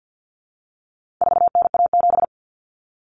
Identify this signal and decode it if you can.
Morse code, heard as a keyed tone, about 48 wpm, 700 Hz. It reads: 4NA7
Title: Common Cut Numbers - 4-Digit 50wpm